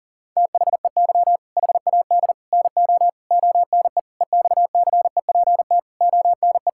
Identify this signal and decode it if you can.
Morse code, heard as a keyed tone, about 40 wpm, 700 Hz. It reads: THEY HAD NO ONE EXCEPT ONE
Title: Sentences from Top 1600 Words 40wpm